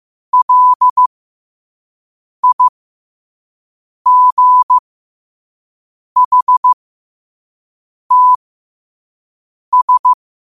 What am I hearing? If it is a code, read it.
Morse code, heard as a keyed tone, about 15 wpm, spaced as for 5 wpm, 1000 Hz. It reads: LIGHTS
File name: qotdfiles / 5wpm